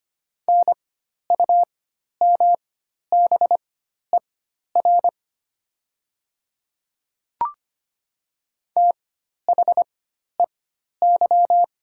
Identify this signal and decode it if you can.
Morse code, heard as a keyed tone, about 25 wpm, spaced as for 12 wpm, 700 Hz. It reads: NUMBER THEY
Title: 4x Character Spacing - Top 100 Words - 25wpm